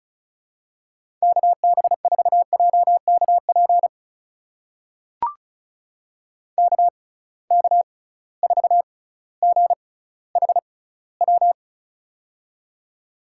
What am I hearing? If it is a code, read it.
Morse code, heard as a keyed tone, about 36 wpm, spaced as for 18 wpm, 700 Hz. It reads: KB4JKP KK4GHW
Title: USA Call Signs - 6 Characters - 6x Character Spacing - 35wpm